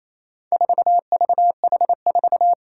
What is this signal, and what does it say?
Morse code, heard as a keyed tone, about 28 wpm, 700 Hz. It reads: 4VH4
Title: Four Challenging Letter-Number Combinations 28wpm